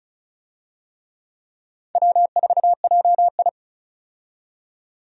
Morse code, heard as a keyed tone, about 36 wpm, 700 Hz. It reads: W4JI
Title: USA Call Signs - 4 Characters - 35wpm